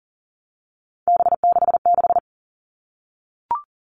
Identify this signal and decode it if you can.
Morse code, heard as a keyed tone, about 40 wpm, 700 Hz. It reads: B66
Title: All Cut Numbers - 3-Digit 40wpm